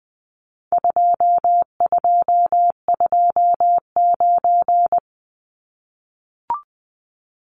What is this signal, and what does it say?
Morse code, heard as a keyed tone, about 20 wpm, 700 Hz. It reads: 2229
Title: Numbers - 4-Digit 20wpm